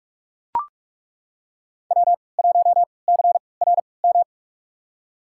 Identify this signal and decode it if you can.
Morse code, heard as a keyed tone, about 42 wpm, spaced as for 25 wpm, 700 Hz. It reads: W1CRM
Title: USA Call Signs - 5 Characters - 3x Character Spacing - 45wpm